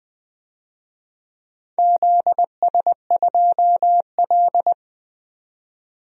Morse code, heard as a keyed tone, about 21 wpm, 700 Hz. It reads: ZS2L
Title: DXCC-100 Call Signs - 4 Characters - 20wpm